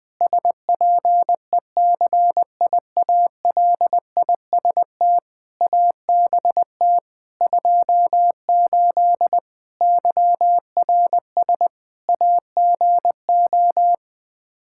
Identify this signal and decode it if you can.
Morse code, heard as a keyed tone, about 20 wpm, 700 Hz. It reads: SPECIALIST ABT 28 YRS AGO
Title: Practice QSO - Exchange 3 - More Info - 20wpm